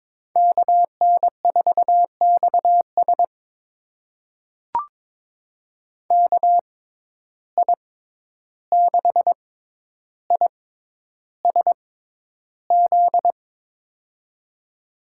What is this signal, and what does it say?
Morse code, heard as a keyed tone, about 22 wpm, spaced as for 11 wpm, 700 Hz. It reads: KN4XS KI6ISZ
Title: USA Call Signs - 6x Character Spacing - 22wpm